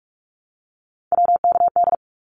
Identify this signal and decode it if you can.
Morse code, heard as a keyed tone, about 44 wpm, 700 Hz. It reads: WKD
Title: Abbreviations 45wpm